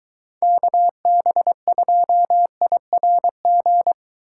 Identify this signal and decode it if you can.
Morse code, heard as a keyed tone, about 23 wpm, 700 Hz. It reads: KB2IRG